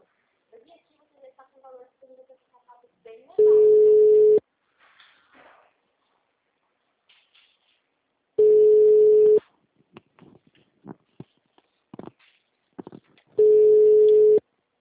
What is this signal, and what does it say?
Ringback tone (Europe, 425 Hz), with speech.
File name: out- -2058- - - .8234